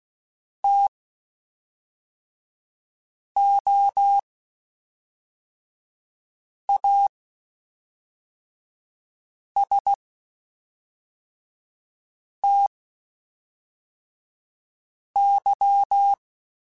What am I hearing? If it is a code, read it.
Morse code, heard as a keyed tone, about 16 wpm, spaced as for 3 wpm, 800 Hz. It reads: TOASTY
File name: Spaced Characters / Spaced Characters-mp3s / Weather5 12 sp @5